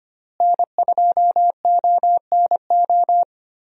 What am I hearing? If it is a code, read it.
Morse code, heard as a keyed tone, about 25 wpm, 700 Hz. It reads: N2ONO
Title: USA Call Signs - 5 Characters - Repeated 3x - 25wpm